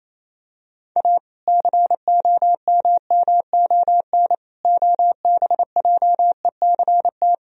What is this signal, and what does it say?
Morse code, heard as a keyed tone, about 28 wpm, 700 Hz. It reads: A COMMON OBJECT
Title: Sentences from Top 1000 Words - Review Part 3 - 28wpm